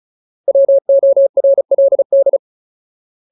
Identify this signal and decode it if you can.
Morse code, heard as a keyed tone, about 35 wpm, 550 Hz. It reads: WORLD
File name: Ultimate Word List / 35WPM Ultimate